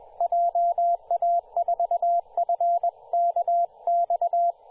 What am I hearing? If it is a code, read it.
Morse code, heard as a keyed tone, about 21 wpm, 690 Hz, with radio noise. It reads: JA4FKX
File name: t2r-18-071mhz-cw